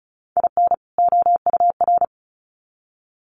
Morse code, heard as a keyed tone, about 34 wpm, 700 Hz. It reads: IN OUR
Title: Sets of 2 Words 35wpm